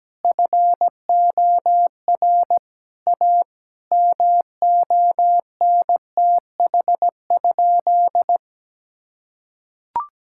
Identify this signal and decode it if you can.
Morse code, heard as a keyed tone, about 17 wpm, 700 Hz. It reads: FOR A MONTH?
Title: Sentences from Top 1000 Words - Review Part 5 - 17wpm